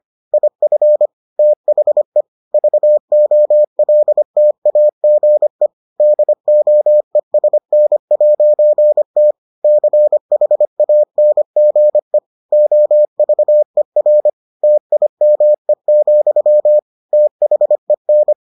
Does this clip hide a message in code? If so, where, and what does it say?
Yes: Morse code from 0.3 to 18.3 s, heard as a keyed tone, about 25 wpm, 600 Hz. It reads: IF THE VOLTAGE DOESN'T CHANGE OVER TIME, THEN